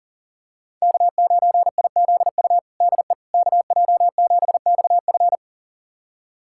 Morse code, heard as a keyed tone, about 40 wpm, 700 Hz. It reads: K9IZU DE KJ7XF
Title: USA Callsign DE Callsign - 5 Characters - 40wpm